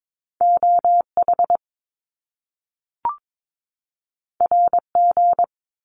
Morse code, heard as a keyed tone, about 22 wpm, 700 Hz. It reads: OH RG
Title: Top Two Letter Combinations in Words 22wpm